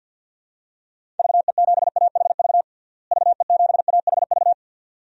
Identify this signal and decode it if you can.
Morse code, heard as a keyed tone, about 52 wpm, 700 Hz. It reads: VE7AHV VE7AHV
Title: DXCC-100 Call Signs - 6 Characters - Repeated 2x - 50wpm